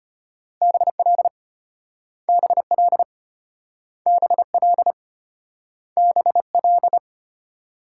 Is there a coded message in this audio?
Yes: Morse code from 0.6 to 7.0 s, heard as a keyed tone, about 34 wpm, 700 Hz. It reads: BL BL BL BL